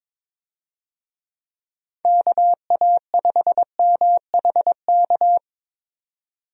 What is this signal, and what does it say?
Morse code, heard as a keyed tone, about 21 wpm, 700 Hz. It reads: KA5MHK
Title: USA Call Signs - 6 Characters - 22wpm